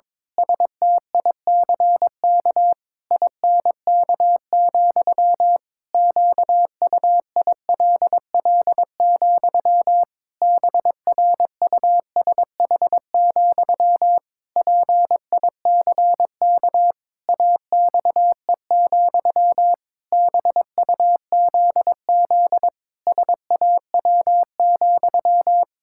Morse code, heard as a keyed tone, about 22 wpm, 700 Hz. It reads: STICK INK, QUILL, BRUSH, PICK AXE, BUZZ SAW,
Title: Pangram Challenge v2 - 22wpm